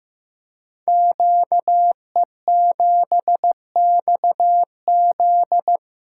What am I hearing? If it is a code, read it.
Morse code, heard as a keyed tone, about 15 wpm, 700 Hz. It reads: QE7XZ
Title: Code Groups - 5 Letter:Numbers - 15wpm